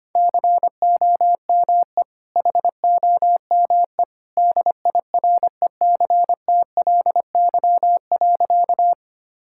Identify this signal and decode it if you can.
Morse code, heard as a keyed tone, about 25 wpm, 700 Hz. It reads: COME HOME DIRECTLY.
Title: Sentences from Top 1000 Words - Review Part 5 - 25wpm